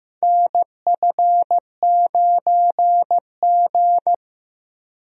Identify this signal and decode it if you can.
Morse code, heard as a keyed tone, about 15 wpm, 700 Hz. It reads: NF9G
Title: USA Call Signs - 4 Characters - Repeated 3x - 15wpm